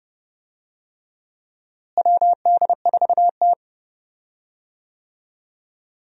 Morse code, heard as a keyed tone, about 32 wpm, 700 Hz. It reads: WD4T
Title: USA Call Signs - 4 Characters - Repeated 3x - 30wpm